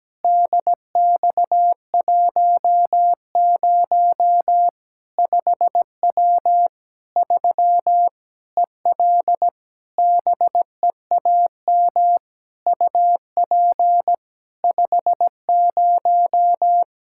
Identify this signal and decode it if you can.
Morse code, heard as a keyed tone, about 17 wpm, 700 Hz. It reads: DX10 5W 3 EL BEAM UP 50
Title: Practice QSO - Exchange 2 - Info - 17wpm